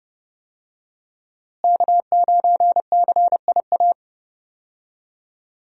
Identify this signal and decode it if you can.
Morse code, heard as a keyed tone, about 28 wpm, 700 Hz. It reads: K9CIA
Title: USA Call Signs - 5 Characters - Repeated 3x - 30wpm